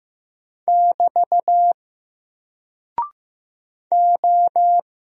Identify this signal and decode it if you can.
Morse code, heard as a keyed tone, about 15 wpm, 700 Hz. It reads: <BT> O
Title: Single Character (Letter:Number:Punctuation:Prosign) 15wpm